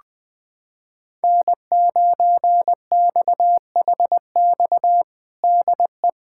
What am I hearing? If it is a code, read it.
Morse code, heard as a keyed tone, about 21 wpm, 700 Hz. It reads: N9XHX DE
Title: USA Callsign DE Callsign - 5 Characters - 20wpm